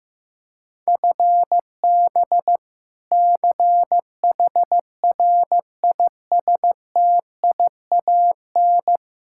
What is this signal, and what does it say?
Morse code, heard as a keyed tone, about 15 wpm, 700 Hz. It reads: FB CHRISTIAN
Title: K1USN SST QSO Practice 15wpm